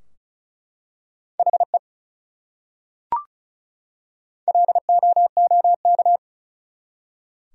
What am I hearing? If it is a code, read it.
Morse code, heard as a keyed tone, about 34 wpm, 700 Hz. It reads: HE LOOK
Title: Top 100 Words 35wpm